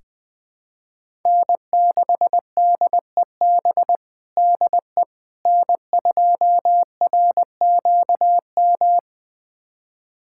Morse code, heard as a keyed tone, about 20 wpm, 700 Hz. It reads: N6DEB DE N2RQM
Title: USA Callsign DE Callsign - 5 Characters - 20wpm